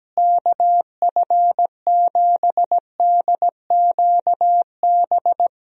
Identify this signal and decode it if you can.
Morse code, heard as a keyed tone, about 17 wpm, 700 Hz. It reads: KF7DQB
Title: USA Call Signs - 6 Characters - Repeated 2x - 17wpm